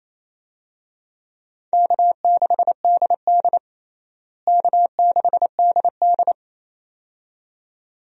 Morse code, heard as a keyed tone, about 28 wpm, 700 Hz. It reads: K6DD K6DD
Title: USA Call Signs - 4 Characters - Repeated 2x - 28wpm